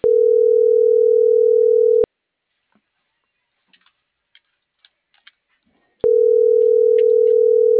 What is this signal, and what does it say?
Ringback tone (North America), 440 + 480 Hz.